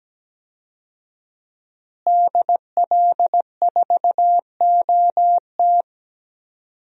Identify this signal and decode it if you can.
Morse code, heard as a keyed tone, about 17 wpm, 700 Hz. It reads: DL4OT
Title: USA Call Signs - 5 Characters - Repeated 3x - 17wpm